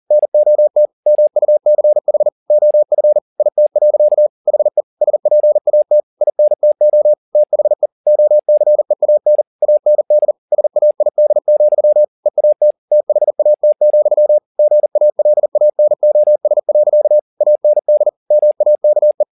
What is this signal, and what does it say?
Morse code, heard as a keyed tone, about 40 wpm, 600 Hz. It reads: NOT MUCH OF IT. HE SPAT INTO THE OCEAN AND SAID, EAT THAT, GALANOS. AND MAKE
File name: the old man and the sea 2 40wpm